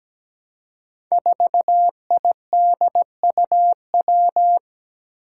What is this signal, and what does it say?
Morse code, heard as a keyed tone, about 17 wpm, 700 Hz. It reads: 4IDUW
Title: Five Challenging Letter-Number Combinations 17wpm